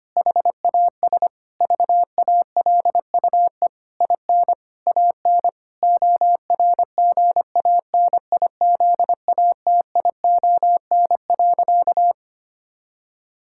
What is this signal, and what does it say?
Morse code, heard as a keyed tone, about 25 wpm, 700 Hz. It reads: HAS VALUE IN AN ORGANIZATION.
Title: Sentences from Top 1000 Words - Review Part 6 - 25wpm